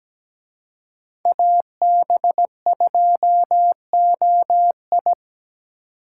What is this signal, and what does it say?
Morse code, heard as a keyed tone, about 20 wpm, 700 Hz. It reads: AB2OI
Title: Speed Racing - USA 2x2 Call Signs 17wpm